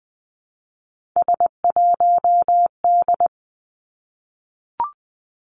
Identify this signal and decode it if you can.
Morse code, heard as a keyed tone, about 20 wpm, 700 Hz. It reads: S1D
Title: Three Challenging Letter-Number Combinations 20wpm